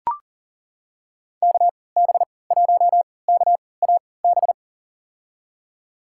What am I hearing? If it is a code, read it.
Morse code, heard as a keyed tone, about 37 wpm, spaced as for 22 wpm, 700 Hz. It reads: KB1KAB
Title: USA Call Signs - 6 Characters - 3x Character Spacing - 40wpm